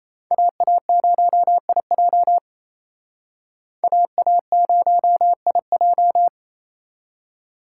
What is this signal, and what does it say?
Morse code, heard as a keyed tone, about 35 wpm, 700 Hz. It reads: AA0IJ AA0IJ
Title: Speed Racing - USA 2x2 Call Signs 28wpm